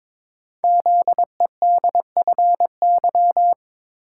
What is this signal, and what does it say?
Morse code, heard as a keyed tone, about 22 wpm, 700 Hz. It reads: ZEDFY
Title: Code Groups - 5 Letters - 22wpm